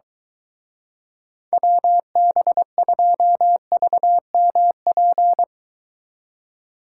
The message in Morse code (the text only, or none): WB2VMP